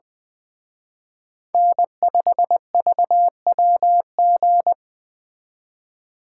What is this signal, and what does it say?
Morse code, heard as a keyed tone, about 24 wpm, 700 Hz. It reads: N5VWG
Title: Speed Racing - USA Call Signs 20wpm